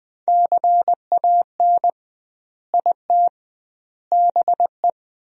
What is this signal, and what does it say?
Morse code, heard as a keyed tone, about 20 wpm, 700 Hz. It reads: CAN IT BE
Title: 2x Word-Spacing - Sentences from Top 500 Words - 20wpm